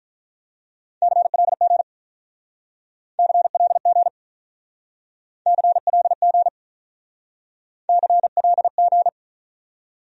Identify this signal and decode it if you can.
Morse code, heard as a keyed tone, about 46 wpm, 700 Hz. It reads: CLG CLG CLG CLG